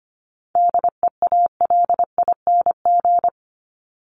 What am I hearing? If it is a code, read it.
Morse code, heard as a keyed tone, about 25 wpm, 700 Hz. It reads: DEALING
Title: ING Suffix - Top 100 Words with Base Words 25wpm